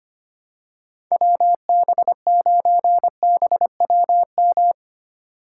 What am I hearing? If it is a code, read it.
Morse code, heard as a keyed tone, about 25 wpm, 700 Hz. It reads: WB9BWM
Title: USA Call Signs - 6 Characters - 4x Character Spacing - 25wpm